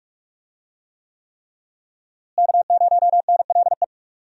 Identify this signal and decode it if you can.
Morse code, heard as a keyed tone, about 46 wpm, 700 Hz. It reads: K0NLE
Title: USA Call Signs - 5 Characters - Repeated 3x - 45wpm